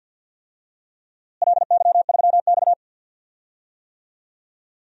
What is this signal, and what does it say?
Morse code, heard as a keyed tone, about 52 wpm, 700 Hz. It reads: LY3X
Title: DXCC-100 Call Signs - 4 Characters - Repeated 3x - 50wpm